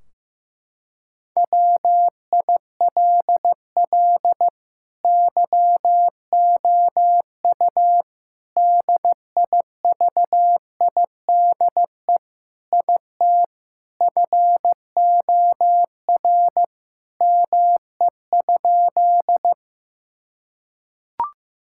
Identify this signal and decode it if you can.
Morse code, heard as a keyed tone, about 15 wpm, 700 Hz. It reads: WILL YOU DIVIDE IT FOR ME?
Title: Sentences from Top 1600 Words 15wpm